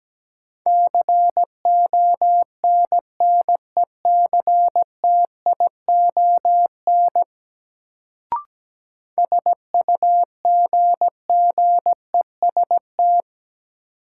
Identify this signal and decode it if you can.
Morse code, heard as a keyed tone, about 17 wpm, 700 Hz. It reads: CONNECTION SUGGEST
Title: Sentences from Top 1300 Words - New Word Drill Alt - 17wpm